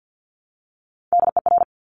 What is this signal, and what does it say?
Morse code, heard as a keyed tone, about 50 wpm, 700 Hz. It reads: DER
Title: Top Three Letter Combinations in Words 50wpm